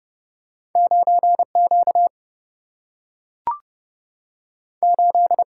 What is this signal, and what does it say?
Morse code, heard as a keyed tone, about 31 wpm, 700 Hz. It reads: 9 Q 8
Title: Single-Letter Single-Number 30wpm